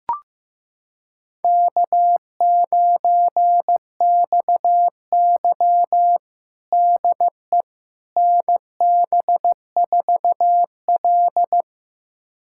Morse code, heard as a keyed tone, about 15 wpm, 700 Hz. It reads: K9XY DE NB4L
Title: USA Callsign DE Callsign - 4 Characters - 15wpm